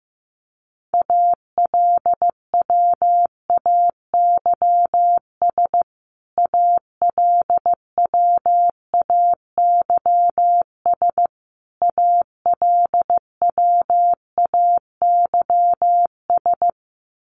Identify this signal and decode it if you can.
Morse code, heard as a keyed tone, about 14 wpm, 700 Hz. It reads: ALWAYS ALWAYS ALWAYS
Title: Top 500 Words Repeated 15wpm